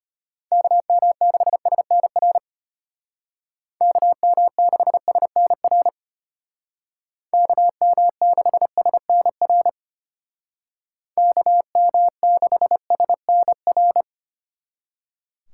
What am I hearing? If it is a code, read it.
Morse code, heard as a keyed tone, about 33 wpm, 700 Hz. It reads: KM6SNR KM6SNR KM6SNR KM6SNR